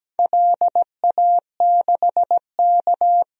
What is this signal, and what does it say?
Morse code, heard as a keyed tone, about 17 wpm, 700 Hz. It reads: LA6K
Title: DXCC-100 Call Signs - 4 Characters - 17wpm